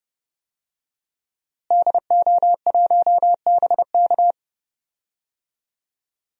Morse code, heard as a keyed tone, about 30 wpm, 700 Hz. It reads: DO1BK
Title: DXCC-100 Call Signs - 5 Characters - Repeated 3x - 30wpm